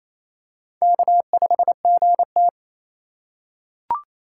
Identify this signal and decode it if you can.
Morse code, heard as a keyed tone, about 28 wpm, 700 Hz. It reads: K5GT
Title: USA Call Signs - 4 Characters - 4x Character Spacing - 28wpm